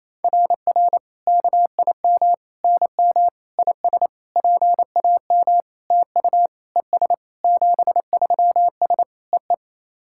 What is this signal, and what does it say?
Morse code, heard as a keyed tone, about 28 wpm, 700 Hz. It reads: RR KIM NM IS PAM TU ES 73S EE
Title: SOTA QSO Practice 28wpm